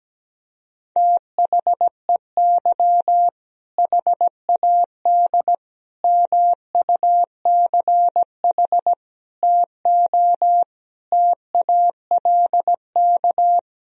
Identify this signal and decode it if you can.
Morse code, heard as a keyed tone, about 17 wpm, 700 Hz. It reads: THEY HAD MUCH TO TALK
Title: Sentences from Top 400 Words and Abbreviations 17wpm